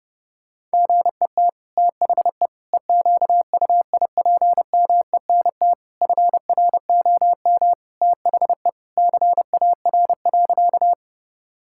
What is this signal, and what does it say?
Morse code, heard as a keyed tone, about 30 wpm, 700 Hz. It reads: GET THE EQUIPMENT FROM THE CAR.
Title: Sentences from Top 1000 Words - Review Part 5 - 30wpm